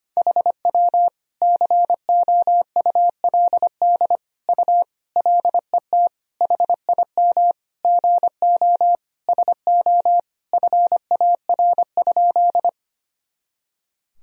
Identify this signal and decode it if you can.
Morse code, heard as a keyed tone, about 25 wpm, 700 Hz. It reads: HW COULD U LET HIM GO SO FAR?
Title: Sentences from Top 300 Words and Abbreviations 25wpm